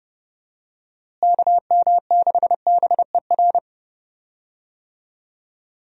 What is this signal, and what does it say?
Morse code, heard as a keyed tone, about 30 wpm, 700 Hz. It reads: KM6BER
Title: USA Call Signs - 6 Characters - Repeated 3x - 30wpm